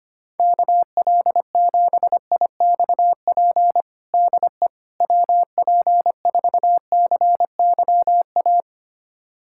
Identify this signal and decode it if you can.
Morse code, heard as a keyed tone, about 25 wpm, 700 Hz. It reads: KL7IXP DE WP4CYA
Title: USA Callsign DE Callsign - 6 Characters - 25wpm